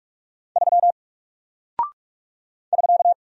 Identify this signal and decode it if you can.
Morse code, heard as a keyed tone, about 44 wpm, 700 Hz. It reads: 3 <SK>
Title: Single Character (Letter:Number:Punctuation:Prosign) 45wpm